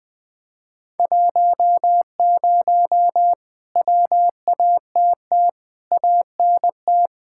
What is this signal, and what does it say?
Morse code, heard as a keyed tone, about 20 wpm, 700 Hz. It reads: 10 WATT ANT
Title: Speed-Racing Common QSO Phrases - 20wpm